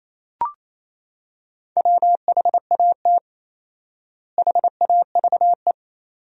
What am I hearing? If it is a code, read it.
Morse code, heard as a keyed tone, about 28 wpm, 700 Hz. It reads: WHAT HAVE
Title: 4x Word-Spacing - Sentences from Top 500 Words - 28wpm